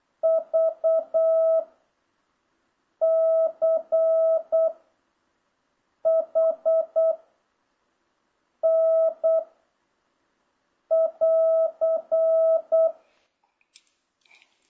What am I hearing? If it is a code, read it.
Morse code, heard as a keyed tone, about 8 wpm, 660 Hz. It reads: V C H N <AR>